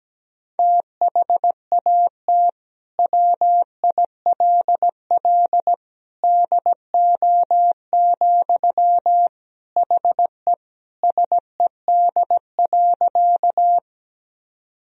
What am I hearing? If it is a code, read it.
Morse code, heard as a keyed tone, about 17 wpm, 700 Hz. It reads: THAT WILL DO, HE SED.
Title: Sentences from Top 100 Words and Abbreviations 17wpm